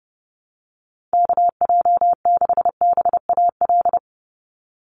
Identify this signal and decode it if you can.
Morse code, heard as a keyed tone, about 30 wpm, 700 Hz. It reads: KJ6BAL
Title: USA Call Signs 30wpm